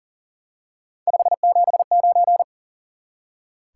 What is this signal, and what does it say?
Morse code, heard as a keyed tone, about 40 wpm, 700 Hz. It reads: 579
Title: Top 180 QSO Elements 40wpm